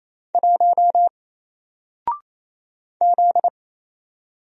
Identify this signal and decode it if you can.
Morse code, heard as a keyed tone, about 28 wpm, 700 Hz. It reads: 1 Z
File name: Single-Letter-Number-28wpm